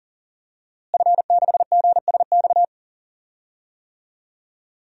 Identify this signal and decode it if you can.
Morse code, heard as a keyed tone, about 40 wpm, 700 Hz. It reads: F6GSX
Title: DXCC-100 Call Signs - 5 Characters - Repeated 2x - 40wpm